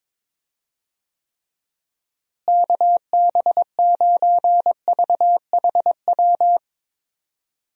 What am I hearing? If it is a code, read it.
Morse code, heard as a keyed tone, about 22 wpm, 700 Hz. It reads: KB9VHW
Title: USA Call Signs - 6 Characters - Repeated 2x - 22wpm